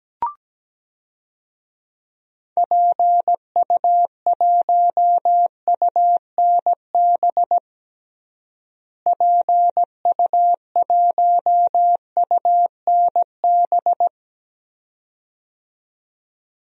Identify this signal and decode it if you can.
Morse code, heard as a keyed tone, about 17 wpm, 700 Hz. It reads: PU1UNB PU1UNB
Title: DXCC-100 Call Signs - 6 Characters - Repeated 2x - 17wpm